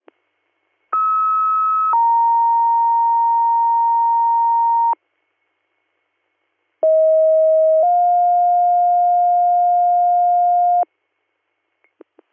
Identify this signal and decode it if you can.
Two-tone page, heard as a one-second tone, then a three-second tone, about 1290 then 930 Hz, about 640 then 710 Hz. It is heard from 0.9 to 10.8 s.